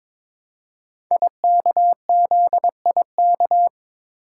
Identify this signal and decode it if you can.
Morse code, heard as a keyed tone, about 22 wpm, 700 Hz. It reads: IKZIK
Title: Code Groups - 5 Letters - 22wpm